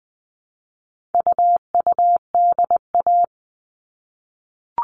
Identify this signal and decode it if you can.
Morse code, heard as a keyed tone, about 20 wpm, 700 Hz. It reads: UUDA
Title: All Cut Numbers - 4-Digit 20wpm